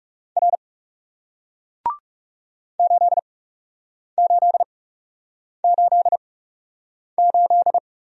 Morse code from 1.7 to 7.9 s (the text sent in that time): 8888